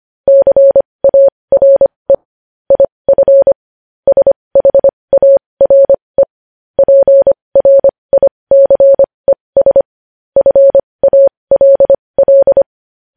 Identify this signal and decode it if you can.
Morse code, heard as a keyed tone, about 25 wpm, spaced as for 20 wpm, 550 Hz. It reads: CARE IF SHARE PRICES FALL
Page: News headlines in morse code at 20 wpm